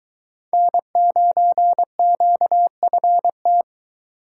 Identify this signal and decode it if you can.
Morse code, heard as a keyed tone, about 23 wpm, 700 Hz. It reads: N9QFT